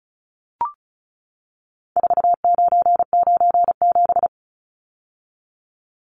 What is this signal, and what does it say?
Morse code, heard as a keyed tone, about 34 wpm, 700 Hz. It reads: 4997
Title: Numbers - 4-Digit 35wpm